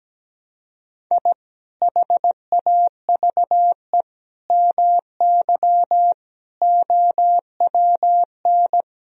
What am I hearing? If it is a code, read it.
Morse code, heard as a keyed tone, about 17 wpm, 700 Hz. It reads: I HAVE MY OWN
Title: Sentences from Top 1000 Words - Review Part 3 - 17wpm